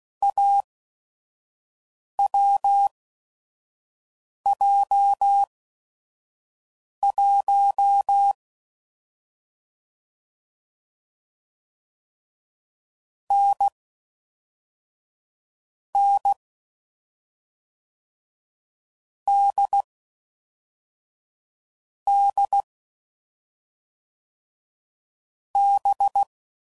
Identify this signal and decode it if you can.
Morse code, heard as a keyed tone, about 16 wpm, spaced as for 5 wpm, 800 Hz. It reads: AWJ1 N N D D B
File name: Mixed / Short Runs / awj1ndb6gz7